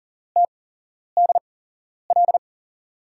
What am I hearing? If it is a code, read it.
Morse code, heard as a keyed tone, about 40 wpm, spaced as for 11 wpm, 700 Hz. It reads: TDL
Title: USA Call Signs - 8x Character Spacing - 40wpm